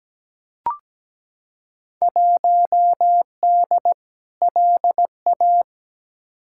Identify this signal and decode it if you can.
Morse code, heard as a keyed tone, about 17 wpm, 700 Hz. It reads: 1D LA
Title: Field Day Exchanges - 17wpm